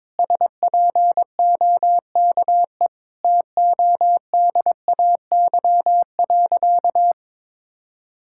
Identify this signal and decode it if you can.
Morse code, heard as a keyed tone, about 21 wpm, 700 Hz. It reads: SPOKE TODAY.
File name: practice / 22wpm / Sentences with 8-Word Phrases - 22wpm